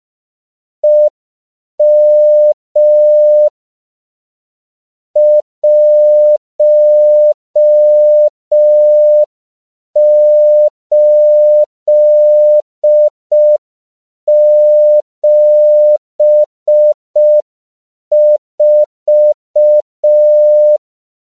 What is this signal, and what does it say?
Morse code, heard as a keyed tone, about 5 wpm, 600 Hz. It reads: EM 1874
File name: CW / 5PPM